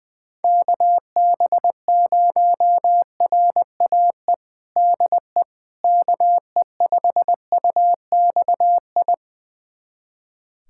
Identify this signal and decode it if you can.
Morse code, heard as a keyed tone, about 20 wpm, 700 Hz. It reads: KB0RAE DE KE5UXI
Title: USA Callsign DE Callsign 20wpm